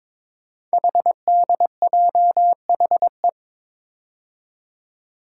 Morse code, heard as a keyed tone, about 21 wpm, 700 Hz. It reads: HDJHE
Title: Code Groups - 5 Letters - 22wpm